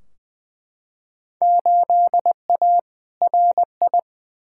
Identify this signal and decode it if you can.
Morse code, heard as a keyed tone, about 20 wpm, 700 Hz. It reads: 8A RI
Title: Field Day Exchanges Encore - 20wpm